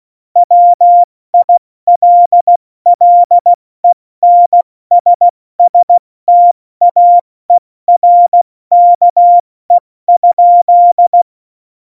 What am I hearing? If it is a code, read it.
Morse code, heard as a keyed tone, about 16 wpm, 700 Hz. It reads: WILLENSSTAERKE?
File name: Morsekurs / Witze / Witz 11. Juni 2019, 70 F80